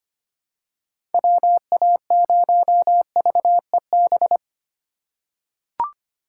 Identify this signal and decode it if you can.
Morse code, heard as a keyed tone, about 25 wpm, 700 Hz. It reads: WA0VEB
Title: USA Call Signs - 6 Characters - 3x Character Spacing - 25wpm